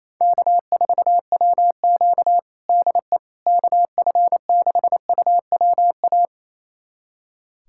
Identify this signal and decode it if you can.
Morse code, heard as a keyed tone, about 28 wpm, 700 Hz. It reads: K4WQ DE KF6UWA
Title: USA Callsign DE Callsign 28wpm